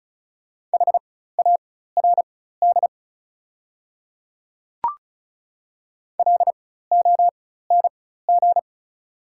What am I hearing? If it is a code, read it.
Morse code, heard as a keyed tone, about 36 wpm, spaced as for 16 wpm, 690 Hz. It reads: HARD LONG
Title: 4x Character Spacing - Top 300 Words - 35wpm